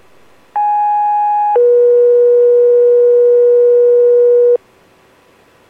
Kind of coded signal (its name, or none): two-tone page